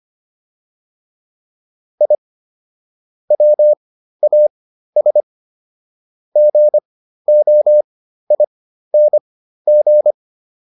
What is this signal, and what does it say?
Morse code, heard as a keyed tone, about 25 wpm, spaced as for 13 wpm, 600 Hz. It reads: I WAS GOING
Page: /intermediate-practice-files/